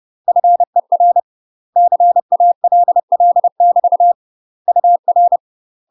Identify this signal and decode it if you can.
Morse code, heard as a keyed tone, about 30 wpm, 700 Hz. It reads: FER CALL <BT> UR
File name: UK MASTER QSOs TRAINING / 30WPM / UK MASTER QSOs 0041